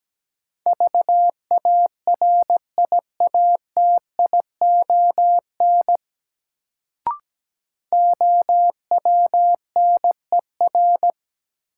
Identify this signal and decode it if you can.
Morse code, heard as a keyed tone, about 17 wpm, 700 Hz. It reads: VARIATION OWNER
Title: Sentences from Top 1600 Words - New Word Drill - 17wpm